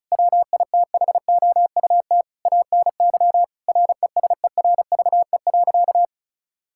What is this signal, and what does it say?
Morse code, heard as a keyed tone, about 36 wpm, 700 Hz. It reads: WITHOUT ANY RESERVE.
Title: Sentences from Top 1000 Words - Review Part 6 - 35wpm